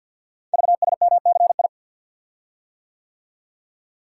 Morse code, heard as a keyed tone, about 50 wpm, 700 Hz. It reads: VSMCI
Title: Code Groups - 5 Letter:Numbers - 50wpm